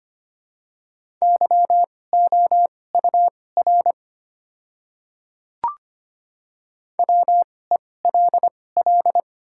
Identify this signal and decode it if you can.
Morse code, heard as a keyed tone, about 25 wpm, spaced as for 18 wpm, 700 Hz. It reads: YOUR WELL
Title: 2x Character Spacing - Top 200 Words - 25wpm